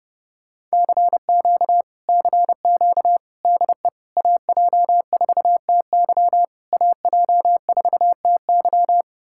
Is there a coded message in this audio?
Yes: Morse code from 0.7 to 9.0 s, heard as a keyed tone, about 33 wpm, 700 Hz. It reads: CQ CQ DE AJ4TY AJ4TY